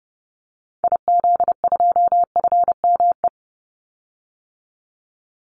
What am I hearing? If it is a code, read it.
Morse code, heard as a keyed tone, about 30 wpm, 700 Hz. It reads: IZ2FME
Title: Call Signs Repeated 3x - 30wpm